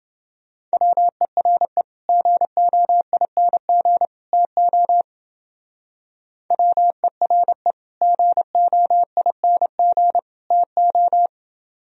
Morse code, heard as a keyed tone, about 28 wpm, 700 Hz. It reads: WERE GOING TO WERE GOING TO